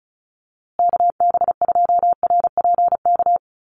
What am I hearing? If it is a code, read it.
Morse code, heard as a keyed tone, about 34 wpm, 700 Hz. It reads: KB2RPK
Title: USA Call Signs Repeated 2x 35wpm